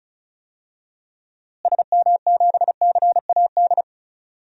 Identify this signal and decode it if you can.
Morse code, heard as a keyed tone, about 34 wpm, 700 Hz. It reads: SM7CAD
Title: DXCC-100 Call Signs - 6 Characters - Repeated 2x - 35wpm